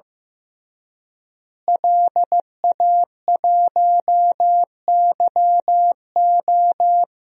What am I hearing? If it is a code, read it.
Morse code, heard as a keyed tone, about 15 wpm, 700 Hz. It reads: LA1YO
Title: DXCC-100 Call Signs - 5 Characters - Repeated 3x - 15wpm